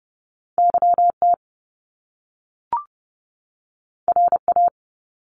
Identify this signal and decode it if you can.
Morse code, heard as a keyed tone, about 30 wpm, 700 Hz. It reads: YT RA
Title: Top Two Letter Combinations in Words 30wpm